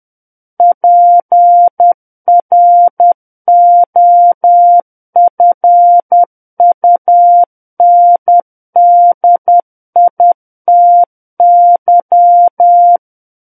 Morse code, heard as a keyed tone, about 10 wpm, 700 Hz. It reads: PROFUNDITY
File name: Book One / 01-Chapter 1 - 10 WPM